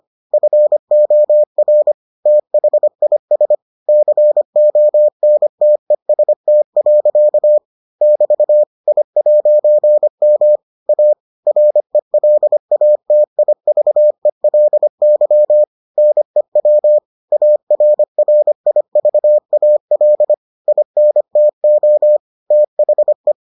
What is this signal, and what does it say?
Morse code, heard as a keyed tone, about 25 wpm, 600 Hz. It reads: FOR THIS CONTEST. <BT> I'M A RELATIVELY NEW ARRIVAL INTO THE